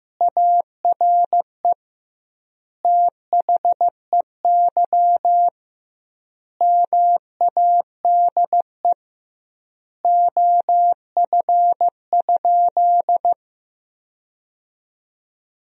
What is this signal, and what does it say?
Morse code, heard as a keyed tone, about 15 wpm, 700 Hz. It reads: ARE THEY MADE OF?
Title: 2x Word-Spacing - Sentences from Top 200 Words - 15wpm